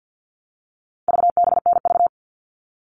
Morse code, heard as a keyed tone, about 48 wpm, 700 Hz. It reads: VBNV
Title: All Cut Numbers - 4-Digit 50wpm